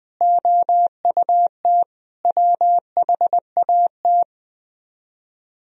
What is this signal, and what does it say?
Morse code, heard as a keyed tone, about 20 wpm, 700 Hz. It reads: OUT WHAT
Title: Speed-Racing - Sets of 5 Words 20wpm